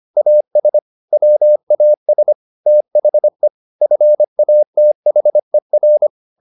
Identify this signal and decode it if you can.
Morse code, heard as a keyed tone, about 25 wpm, 600 Hz. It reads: AS WAS THE FATHER